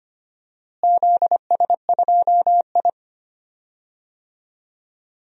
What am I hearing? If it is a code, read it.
Morse code, heard as a keyed tone, about 25 wpm, 700 Hz. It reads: ZS2I
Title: DXCC-100 Call Signs - 4 Characters - Repeated 3x - 25wpm